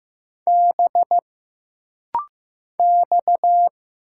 Morse code, heard as a keyed tone, about 15 wpm, 700 Hz. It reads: B X
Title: Single Letters - Rapid-Fire 15wpm